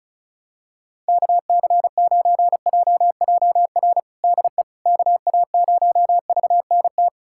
Morse code, heard as a keyed tone, about 34 wpm, 700 Hz. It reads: KC9JJR DE KA0VNT
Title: USA Callsign DE Callsign - 6 Characters - 35wpm